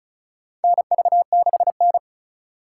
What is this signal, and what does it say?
Morse code, heard as a keyed tone, about 36 wpm, 700 Hz. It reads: NV6N
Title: USA Call Signs - 4x Character Spacing - 35wpm